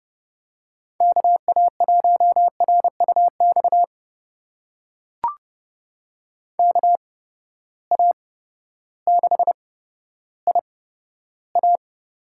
Morse code, heard as a keyed tone, about 30 wpm, spaced as for 14 wpm, 700 Hz. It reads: KA1RUX KA6IA
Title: USA Call Signs - 6 Characters - 8x Character Spacing - 30wpm